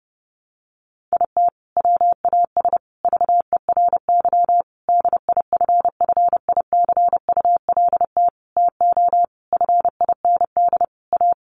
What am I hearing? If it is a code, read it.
Morse code, heard as a keyed tone, about 30 wpm, 700 Hz. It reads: IT WAS VERY DIFFICULT TO FIND A
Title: Sentences from Top 1100 Words 30wpm